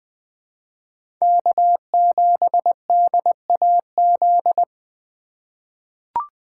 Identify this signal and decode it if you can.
Morse code, heard as a keyed tone, about 21 wpm, 700 Hz. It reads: K7DAZ
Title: USA Call Signs - 5 Characters - 2x Character Spacing - 20wpm